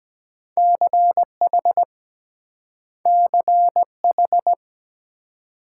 Morse code, heard as a keyed tone, about 23 wpm, 700 Hz. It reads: CH CH